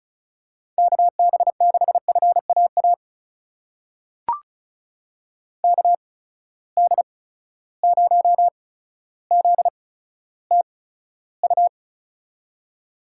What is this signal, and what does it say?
Morse code, heard as a keyed tone, about 34 wpm, spaced as for 15 wpm, 700 Hz. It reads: KB6FAA KD0ZTU
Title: USA Call Signs - 6 Characters - 8x Character Spacing - 35wpm